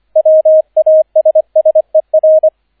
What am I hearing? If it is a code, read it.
Morse code, heard as a keyed tone, about 25 wpm, 630 Hz, with radio noise. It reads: WASSER
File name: 003 Dritter Streich 25wpm